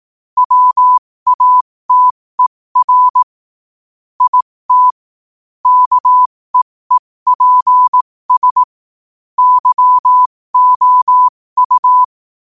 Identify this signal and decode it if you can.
Morse code, heard as a keyed tone, about 18 wpm, spaced as for 15 wpm, 1000 Hz. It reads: WATER IT KEEPS YOU